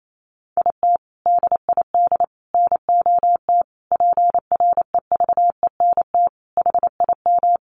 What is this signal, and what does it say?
Morse code, heard as a keyed tone, about 28 wpm, 700 Hz. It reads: IT DID NOT PREVENT HIM
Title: Sentences from Top 1500 Words 28wpm